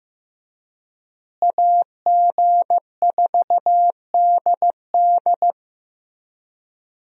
Morse code, heard as a keyed tone, about 14 wpm, 700 Hz. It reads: AG4DD
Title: USA Call Signs - 5 Characters - Repeated 3x - 15wpm